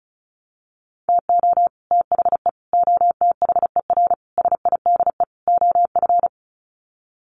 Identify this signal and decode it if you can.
Morse code, heard as a keyed tone, about 34 wpm, 700 Hz. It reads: TO THE OTHER SIDE OF
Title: Sets of 5 Words Encore 35wpm